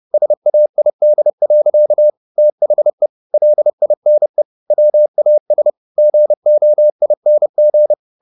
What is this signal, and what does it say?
Morse code, heard as a keyed tone, about 30 wpm, 600 Hz. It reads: SAID. THE LINE WAS GOING
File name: the old man and the sea / the old man and the sea 1 30wpm